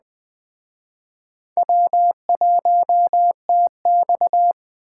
Morse code, heard as a keyed tone, about 24 wpm, 700 Hz. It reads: W1TX
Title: Speed Racing - USA 1x2 Call Signs 20wpm